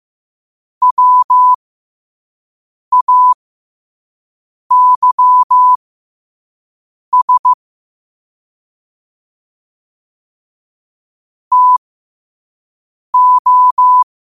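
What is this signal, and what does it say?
Morse code, heard as a keyed tone, about 15 wpm, spaced as for 5 wpm, 1000 Hz. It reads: WAYS TO
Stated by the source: Morse code quote practice at 5 words per minute.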